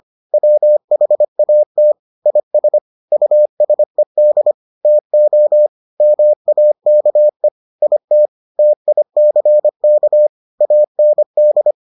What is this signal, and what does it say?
Morse code, heard as a keyed tone, about 25 wpm, 600 Hz. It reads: WHAT IS USED TO MAKE IT TICK AND